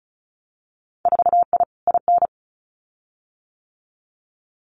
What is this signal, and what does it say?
Morse code, heard as a keyed tone, about 36 wpm, 700 Hz. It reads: 4I IN
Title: Winter Field Day Exchanges 35wpm